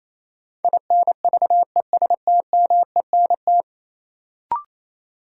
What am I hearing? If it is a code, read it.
Morse code, heard as a keyed tone, about 28 wpm, 700 Hz. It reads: INVESTMENT
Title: Sentences from Top 700 Words - New Word Drill - 28wpm